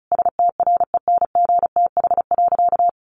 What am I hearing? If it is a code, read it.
Morse code, heard as a keyed tone, about 34 wpm, 700 Hz. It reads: STRENGTH.
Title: Sentences from Top 1300 Words 35wpm